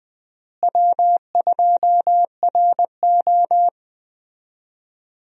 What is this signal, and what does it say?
Morse code, heard as a keyed tone, about 20 wpm, 700 Hz. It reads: W2RO
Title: USA Call Signs - 4 Characters - Repeated 3x - 20wpm